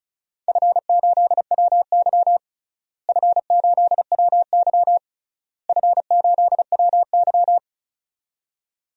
Morse code, heard as a keyed tone, about 34 wpm, 700 Hz. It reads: F8WY F8WY F8WY
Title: DXCC-100 Call Signs - 4 Characters - Repeated 3x - 35wpm